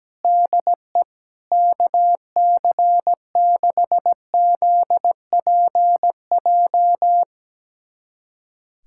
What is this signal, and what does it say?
Morse code, heard as a keyed tone, about 17 wpm, 700 Hz. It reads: DE KC6ZPJ
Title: USA Callsign DE Callsign - 6 Characters - 17wpm